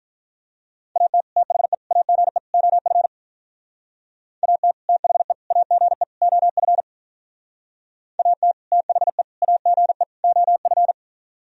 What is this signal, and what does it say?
Morse code, heard as a keyed tone, about 44 wpm, 700 Hz. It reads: AT THE AGE OF AT THE AGE OF AT THE AGE OF